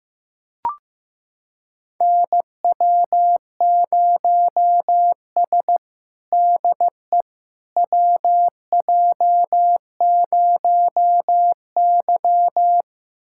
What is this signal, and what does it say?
Morse code, heard as a keyed tone, about 16 wpm, 700 Hz. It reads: NW0S DE WJ0Y
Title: USA Callsign DE Callsign - 4 Characters - 15wpm